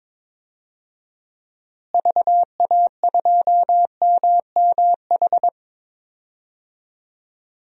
Morse code, heard as a keyed tone, about 22 wpm, 700 Hz. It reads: VA2MMH
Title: DXCC-100 Call Signs - 6 Characters - Repeated 2x - 22wpm